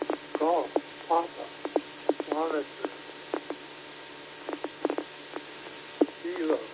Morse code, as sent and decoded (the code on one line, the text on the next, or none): none